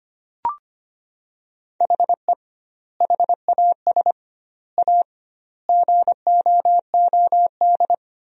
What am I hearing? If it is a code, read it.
Morse code, heard as a keyed tone, about 25 wpm, 700 Hz. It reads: HE HAS A GOOD
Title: 2x Word-Spacing - Sentences from Top 500 Words - 25wpm